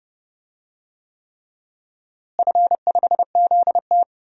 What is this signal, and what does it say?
Morse code, heard as a keyed tone, about 30 wpm, 700 Hz. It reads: F5ZT
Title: DXCC-100 Call Signs - 4 Characters - Repeated 3x - 30wpm